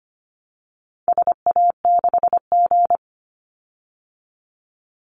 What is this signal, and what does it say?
Morse code, heard as a keyed tone, about 25 wpm, 700 Hz. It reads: SA6G
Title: Call Signs - 25wpm